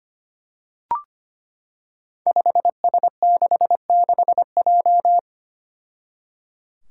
Morse code, heard as a keyed tone, about 25 wpm, 700 Hz. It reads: 5S66J
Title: Five Challenging Letter-Number Combinations 25wpm